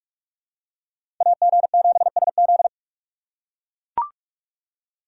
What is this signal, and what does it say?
Morse code, heard as a keyed tone, about 41 wpm, 700 Hz. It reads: AG7SZ
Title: USA Call Signs - 5 Characters - 4x Character Spacing - 45wpm